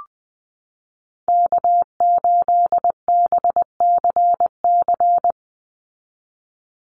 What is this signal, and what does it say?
Morse code, heard as a keyed tone, about 20 wpm, 700 Hz. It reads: K8BCC
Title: USA Call Signs 20wpm